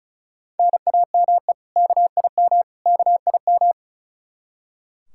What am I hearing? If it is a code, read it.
Morse code, heard as a keyed tone, about 36 wpm, 700 Hz. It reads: NAME KIM KIM
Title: K1USN SST QSO Practice 35wpm